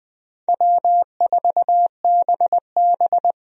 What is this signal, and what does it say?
Morse code, heard as a keyed tone, about 20 wpm, 700 Hz. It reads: W4BB
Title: USA Call Signs - 4 Characters - 20wpm